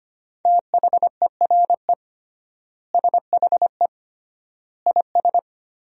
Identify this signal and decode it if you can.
Morse code, heard as a keyed tone, about 25 wpm, 700 Hz. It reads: THERE SHE IS
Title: 3x Word-Spacing - Sentences from Top 200 Words - 25wpm